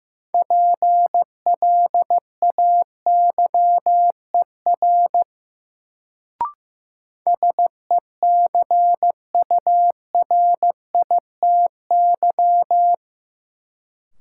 Morse code, heard as a keyed tone, about 15 wpm, 700 Hz. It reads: PLAYER SECURITY
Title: Sentences from Top 1000 Words - Review Part 4 - New Word Drill - 15wpm